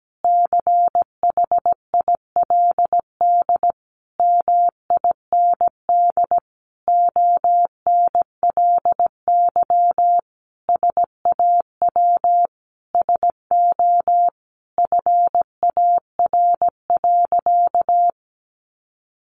Morse code, heard as a keyed tone, about 17 wpm, 700 Hz. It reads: CHILD MIND ONLY SAW SO FAR.
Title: Sentences from Top 800 Words 17wpm